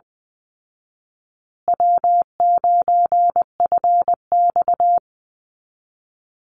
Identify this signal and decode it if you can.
Morse code, heard as a keyed tone, about 20 wpm, 700 Hz. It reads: W9FX
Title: Easy Call Signs Repeated 2x - 20wpm